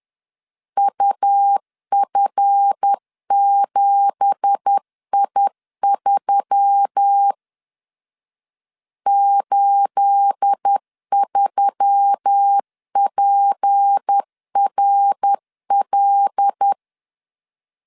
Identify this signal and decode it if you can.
Morse code, heard as a keyed tone, about 10 wpm, 800 Hz. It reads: UF7I3 83PRL
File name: LEZIONI / 9 2867 10wpm 2733